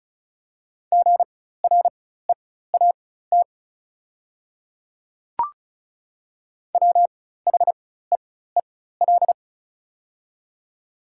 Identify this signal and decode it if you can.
Morse code, heard as a keyed tone, about 35 wpm, spaced as for 16 wpm, 700 Hz. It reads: GREAT WHEEL
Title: 4x Character Spacing - Top 500 Words - 35wpm